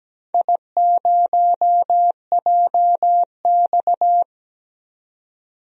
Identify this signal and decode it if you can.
Morse code, heard as a keyed tone, about 17 wpm, 700 Hz. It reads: I0JX
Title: DXCC-100 Call Signs - 4 Characters - 17wpm